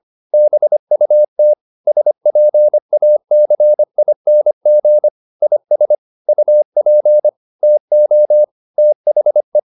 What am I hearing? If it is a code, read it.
Morse code, heard as a keyed tone, about 25 wpm, 600 Hz. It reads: BUT SPACING IS UP TO THE